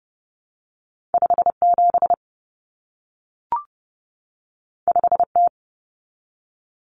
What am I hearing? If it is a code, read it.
Morse code, heard as a keyed tone, about 30 wpm, 700 Hz. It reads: 57 5T
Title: Common Cut Numbers - 2-Digit 30wpm